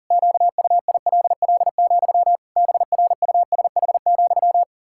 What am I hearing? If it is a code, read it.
Morse code, heard as a keyed tone, about 40 wpm, 700 Hz. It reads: QUILL, BRUSH,
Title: Pangram Challenge v2 - 40wpm